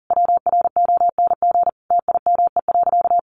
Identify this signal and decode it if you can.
Morse code, heard as a keyed tone, about 40 wpm, 700 Hz. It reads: WRONG TIME.
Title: Sentences from Top 900 Words 40wpm